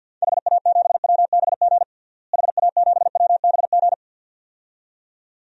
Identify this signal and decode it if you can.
Morse code, heard as a keyed tone, about 51 wpm, 700 Hz. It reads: HA7WBG HA7WBG
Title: DXCC-100 Call Signs - 6 Characters - Repeated 2x - 50wpm